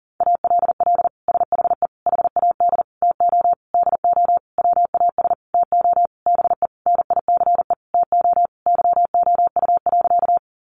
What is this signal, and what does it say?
Morse code, heard as a keyed tone, about 40 wpm, 700 Hz. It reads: ALL SHE HAD TO DO WAS TO BE NICE TO YOU.
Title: Sentences from Top 1100 Words 40wpm